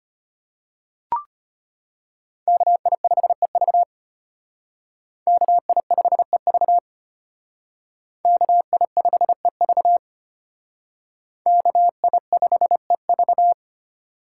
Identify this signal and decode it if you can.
Morse code, heard as a keyed tone, about 32 wpm, 700 Hz. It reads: KI5EV KI5EV KI5EV KI5EV